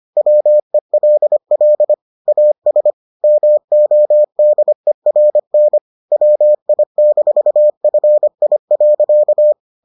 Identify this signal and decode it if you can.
Morse code, heard as a keyed tone, about 25 wpm, 600 Hz. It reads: WELL AS MODERN WI-FI.